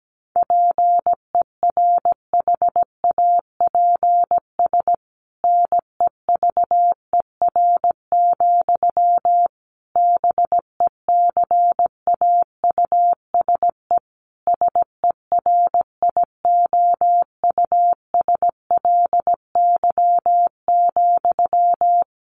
Morse code, heard as a keyed tone, about 17 wpm, 700 Hz. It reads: PERHAPS NEVER, BECAUSE SERIOUSLY,
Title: Sentences from Top 1200 Words 17wpm